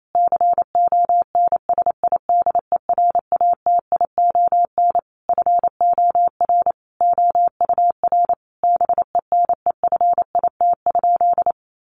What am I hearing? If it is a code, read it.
Morse code, heard as a keyed tone, about 28 wpm, 700 Hz. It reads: CONSIDERATION FOR OUR BENEFIT?
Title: Sentences from Top 1500 Words 28wpm